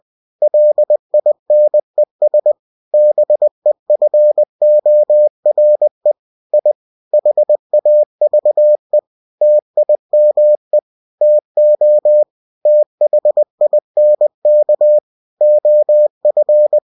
Morse code, heard as a keyed tone, about 20 wpm, 600 Hz. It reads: LINES BEFORE I HAVE TIME TO THINK OF